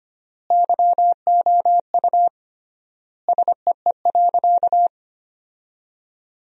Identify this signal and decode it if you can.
Morse code, heard as a keyed tone, about 25 wpm, 700 Hz. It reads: YOU SEE.
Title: 3x Word-Spacing - Sentences from Top 100 Words - 25wpm